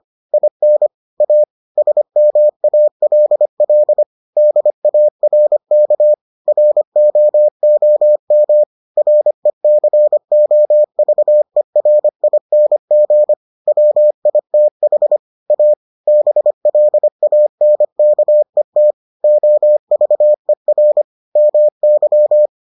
Morse code, heard as a keyed tone, about 25 wpm, 600 Hz. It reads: IN A SMALL DARK ROOM RECOVERING WITH A BLANKET OVER MY